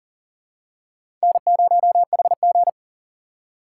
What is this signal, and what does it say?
Morse code, heard as a keyed tone, about 40 wpm, 700 Hz. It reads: N0HG
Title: USA Call Signs - 4 Characters - 2x Character Spacing - 40wpm